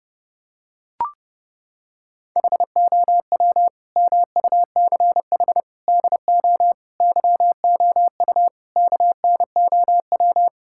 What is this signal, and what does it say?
Morse code, heard as a keyed tone, about 30 wpm, 700 Hz. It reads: HOW MUCH DO YOU KNOW
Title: Sentences from Top 1000 Words - Review Part 6 - 30wpm